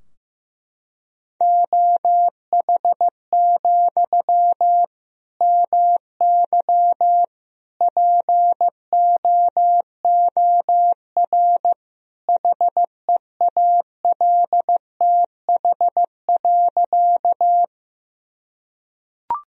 Morse code, heard as a keyed tone, about 15 wpm, 700 Hz. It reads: OH, MY POOR HEALTH.
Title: Sentences from Top 1000 Words - Review Part 3 - 15wpm